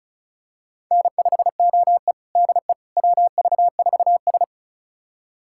Morse code, heard as a keyed tone, about 36 wpm, 700 Hz. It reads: N5OE DE WV4S
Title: USA Callsign DE Callsign - 4 Characters - 35wpm